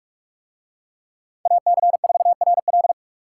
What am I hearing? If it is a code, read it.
Morse code, heard as a keyed tone, about 44 wpm, 700 Hz. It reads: AC4RL